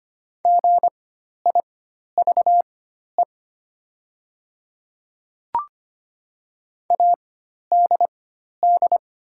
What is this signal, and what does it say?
Morse code, heard as a keyed tone, about 25 wpm, spaced as for 12 wpm, 700 Hz. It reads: GIVE ADD
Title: 4x Character Spacing - Top 300 Words - 25wpm